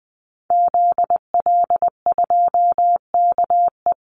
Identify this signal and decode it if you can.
Morse code, heard as a keyed tone, about 20 wpm, 700 Hz. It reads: ZL2KE
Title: Call Signs Repeated 2x - 20wpm